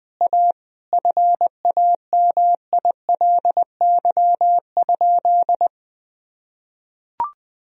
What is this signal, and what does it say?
Morse code, heard as a keyed tone, about 20 wpm, 700 Hz. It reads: A FAMILY?
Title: Sentences from Top 1000 Words - Review Part 4 - 20wpm